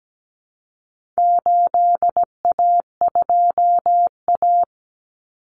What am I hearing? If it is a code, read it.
Morse code, heard as a keyed tone, about 17 wpm, 700 Hz. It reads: 8A2A
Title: Common Cut Numbers - 4-Digit 17wpm